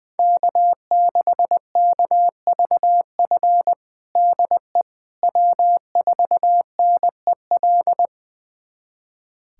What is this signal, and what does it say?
Morse code, heard as a keyed tone, about 20 wpm, 700 Hz. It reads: K6KVF DE W4NEL
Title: USA Callsign DE Callsign - 5 Characters - 20wpm